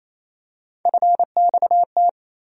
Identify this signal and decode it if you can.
Morse code, heard as a keyed tone, about 28 wpm, 700 Hz. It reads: FXT
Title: USA Call Sign Postfixes - 3 Letters 28wpm